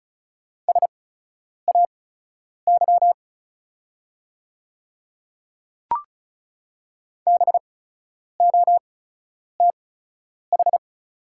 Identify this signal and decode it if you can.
Morse code, heard as a keyed tone, about 36 wpm, spaced as for 10 wpm, 690 Hz. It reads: SAY BOTH
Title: 8x Character Spacing - Top 500 Words - 35wpm